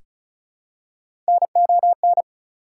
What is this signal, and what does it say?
Morse code, heard as a keyed tone, about 36 wpm, 700 Hz. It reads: NON
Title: Prefixes and Suffixes 35wpm